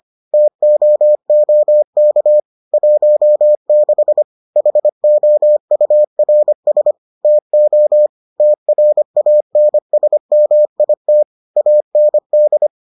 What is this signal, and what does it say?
Morse code, heard as a keyed tone, about 25 wpm, 600 Hz. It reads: TOOK 16 HOURS TO TRANSMIT AND